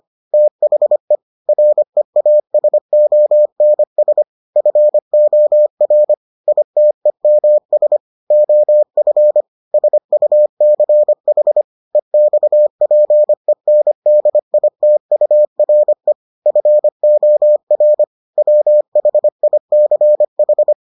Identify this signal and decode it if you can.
Morse code, heard as a keyed tone, about 25 wpm, 600 Hz. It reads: THE REASONS FOR ITEMS OF SUCH EXPENDITURE FOR WHICH